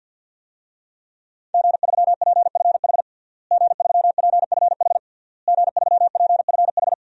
Morse code, heard as a keyed tone, about 52 wpm, 700 Hz. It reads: G3PFH G3PFH G3PFH
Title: DXCC-100 Call Signs - 5 Characters - Repeated 3x - 50wpm